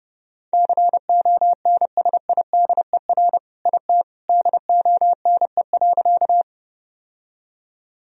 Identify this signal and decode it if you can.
Morse code, heard as a keyed tone, about 30 wpm, 700 Hz. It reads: CONSIDER IT DONE.
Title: Sentences from Top 1000 Words - Review Part 4 - 30wpm